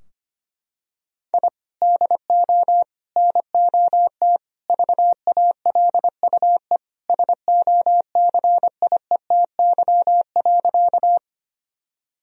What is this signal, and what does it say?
Morse code, heard as a keyed tone, about 25 wpm, 700 Hz. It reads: I DO NOT VALUE SOCIETY.
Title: Sentences from Top 1000 Words - Review Part 4 - 25wpm